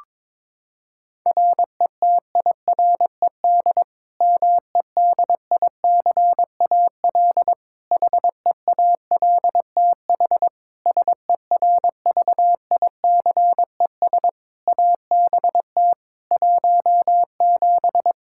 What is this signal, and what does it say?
Morse code, heard as a keyed tone, about 21 wpm, 700 Hz. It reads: RETIRED MEDICAL HEALTH SERVICES ABT 17
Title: K1USN SST QSO Practice 22wpm